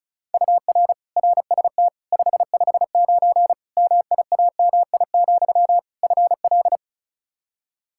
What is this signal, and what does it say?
Morse code, heard as a keyed tone, about 34 wpm, 700 Hz. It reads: UR RST 559 MIAMI, FL
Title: K1USN SST QSO Practice 35wpm